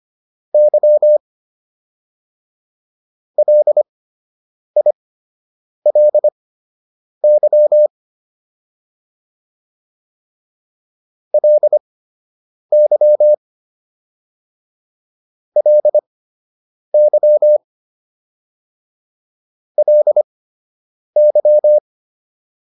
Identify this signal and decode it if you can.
Morse code, heard as a keyed tone, about 25 wpm, spaced as for 8 wpm, 600 Hz. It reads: Y LILY LY LY LY